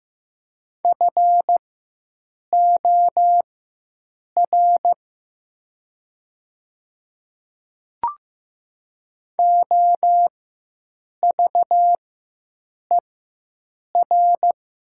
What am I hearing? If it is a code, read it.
Morse code, heard as a keyed tone, about 15 wpm, spaced as for 7 wpm, 700 Hz. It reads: FOR OVER
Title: 4x Character Spacing - Top 100 Words - 15wpm